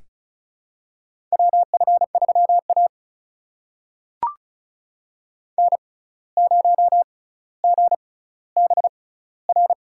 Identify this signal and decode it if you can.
Morse code, heard as a keyed tone, about 34 wpm, spaced as for 17 wpm, 700 Hz. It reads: WF3A N0GBR
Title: USA Call Signs - 6x Character Spacing - 35wpm